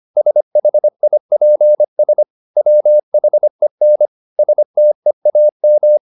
Morse code, heard as a keyed tone, about 25 wpm, 600 Hz. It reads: SHIPS WHEN STEAM